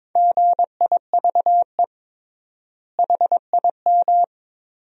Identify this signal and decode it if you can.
Morse code, heard as a keyed tone, about 22 wpm, 700 Hz. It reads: GIVE HIM
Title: 3x Word-Spacing - Sentences from Top 200 Words - 22wpm